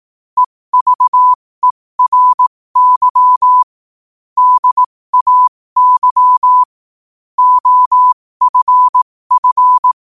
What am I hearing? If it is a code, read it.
Morse code, heard as a keyed tone, about 18 wpm, spaced as for 15 wpm, 1000 Hz. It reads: EVERY DAY OFF
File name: qotdfiles / 15wpm